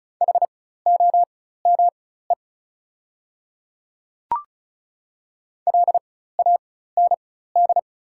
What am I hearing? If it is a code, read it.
Morse code, heard as a keyed tone, about 34 wpm, spaced as for 16 wpm, 690 Hz. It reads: HOME LAND
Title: 4x Character Spacing - Top 200 Words - 35wpm